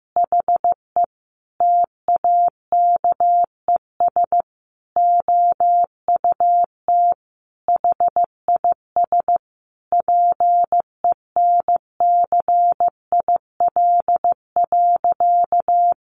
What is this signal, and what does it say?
Morse code, heard as a keyed tone, about 15 wpm, 700 Hz. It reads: HE TAKES OUT HIS PENCIL.
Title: Additional Easy Phrases 15wpm